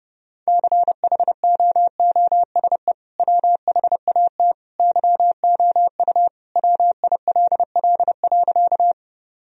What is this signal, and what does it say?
Morse code, heard as a keyed tone, about 30 wpm, 700 Hz. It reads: CHOOSE WHAT YOU WILL.
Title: Sentences from Top 1000 Words - Review Part 4 - 30wpm